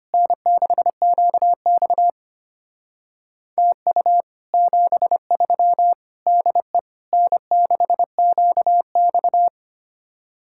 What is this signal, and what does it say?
Morse code, heard as a keyed tone, about 31 wpm, 700 Hz. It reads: N6QX TU 73 DE N6QX
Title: Speed-Racing Common QSO Phrases Encore - 25wpm